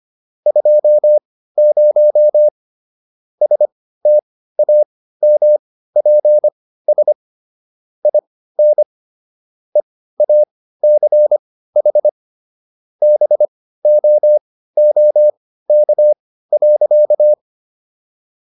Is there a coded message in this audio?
Yes: Morse code from 0.5 to 17.3 s, heard as a keyed tone, about 25 wpm, spaced as for 15 wpm, 600 Hz. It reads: 20 STAMPS IN EACH BOOK.